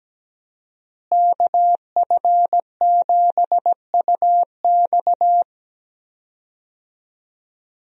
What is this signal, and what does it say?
Morse code, heard as a keyed tone, about 17 wpm, 700 Hz. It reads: KF7UX
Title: USA Call Signs - 5 Characters - Repeated 3x - 17wpm